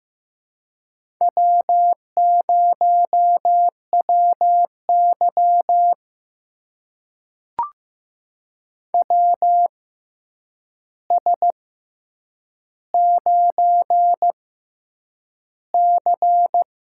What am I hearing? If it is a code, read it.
Morse code, heard as a keyed tone, about 15 wpm, spaced as for 8 wpm, 700 Hz. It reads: W0WY WS9C
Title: USA Call Signs - 4 Characters - 6x Character Spacing - 15wpm